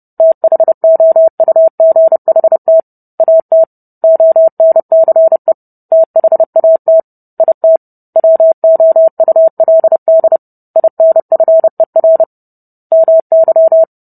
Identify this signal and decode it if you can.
Morse code, heard as a keyed tone, about 30 wpm, 650 Hz. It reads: THOUGHT AT ONCE THAT IT WOULD INFER MY